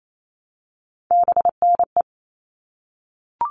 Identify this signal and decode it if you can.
Morse code, heard as a keyed tone, about 28 wpm, 700 Hz. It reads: BNE
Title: All Cut Numbers - 3-Digit 28wpm